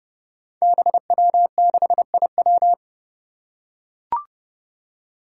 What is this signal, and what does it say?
Morse code, heard as a keyed tone, about 30 wpm, 700 Hz. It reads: BW6IW
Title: Five Challenging Letter-Number Combinations 30wpm